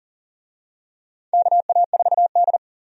Morse code, heard as a keyed tone, about 40 wpm, 700 Hz. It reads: KA4D
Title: USA Call Signs - 4 Characters - 40wpm